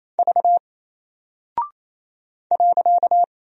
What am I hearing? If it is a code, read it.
Morse code, heard as a keyed tone, about 27 wpm, 700 Hz. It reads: V .
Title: Single Character (Letter:Number:Punctuation:Prosign) Alt 28wpm